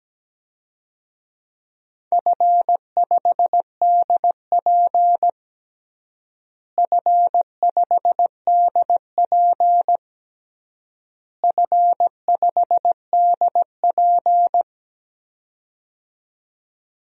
Morse code, heard as a keyed tone, about 17 wpm, 700 Hz. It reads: F5DP F5DP F5DP
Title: DXCC-100 Call Signs - 4 Characters - Repeated 3x - 17wpm